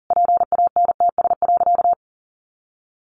Morse code, heard as a keyed tone, about 40 wpm, 700 Hz. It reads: PANTS.
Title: A Sound Of Thunder 40wpm